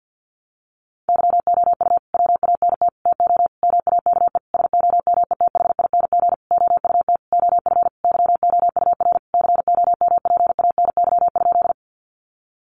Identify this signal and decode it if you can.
Morse code, heard as a keyed tone, about 48 wpm, 700 Hz. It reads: YOU WANT TO MAKE SOMETHING OUT OF YOUR COMPANY?
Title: Sentences from Top 600 Words 50wpm